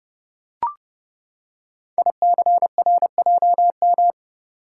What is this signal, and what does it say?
Morse code, heard as a keyed tone, about 30 wpm, 700 Hz. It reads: ICRJM
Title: Code Groups - 5 Letters - 30wpm